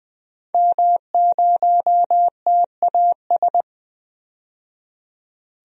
Morse code, heard as a keyed tone, about 20 wpm, 700 Hz. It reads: M0TAS
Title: DXCC-100 Call Signs - 5 Characters - 20wpm